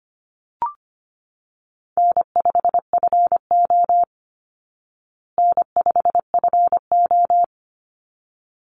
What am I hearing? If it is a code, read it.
Morse code, heard as a keyed tone, about 25 wpm, 700 Hz. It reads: N5FO N5FO
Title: Easy Call Signs Repeated 2x - 25wpm